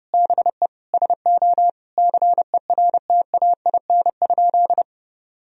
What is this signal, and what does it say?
Morse code, heard as a keyed tone, about 30 wpm, 700 Hz. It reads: BE SO CERTAIN?
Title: Sentences from Top 1000 Words - Review Part 2 - 30wpm